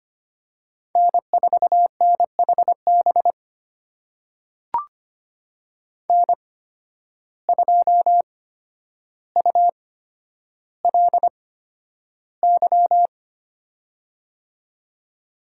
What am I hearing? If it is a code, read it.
Morse code, heard as a keyed tone, about 25 wpm, spaced as for 11 wpm, 700 Hz. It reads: N4NHB N2ULY
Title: USA Call Signs - 5 Characters - 8x Character Spacing - 25wpm